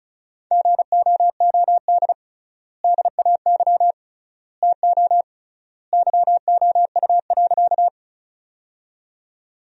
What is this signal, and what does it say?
Morse code, heard as a keyed tone, about 34 wpm, 700 Hz. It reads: GOOD DAY TO YOU.
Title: 3x Word-Spacing - Sentences from Top 200 Words - 35wpm